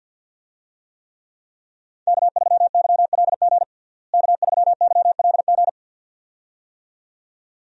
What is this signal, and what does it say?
Morse code, heard as a keyed tone, about 46 wpm, 700 Hz. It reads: K3YLG K3YLG
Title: USA Call Signs - 5 Characters - Repeated 2x - 50wpm